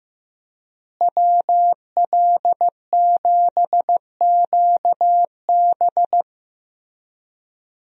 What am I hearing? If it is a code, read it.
Morse code, heard as a keyed tone, about 15 wpm, 700 Hz. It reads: WL7QB
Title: USA Call Signs - 5 Characters - Repeated 2x - 15wpm